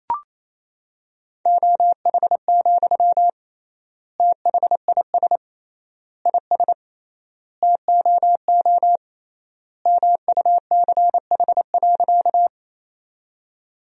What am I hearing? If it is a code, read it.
Morse code, heard as a keyed tone, about 28 wpm, 700 Hz. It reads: OH, THIS IS TOO MUCH.
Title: 3x Word-Spacing - Sentences from Top 500 Words - 28wpm